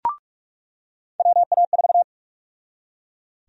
Morse code, heard as a keyed tone, about 42 wpm, 700 Hz. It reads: WA4
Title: USA Call Sign Prefixes - 45wpm